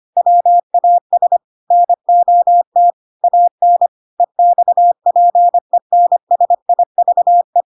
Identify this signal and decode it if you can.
Morse code, heard as a keyed tone, about 25 wpm, 700 Hz. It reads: WAS NOT AN EXPENSIVE
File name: Pinocchio / Pinocchio ch1 25 25 ik1hge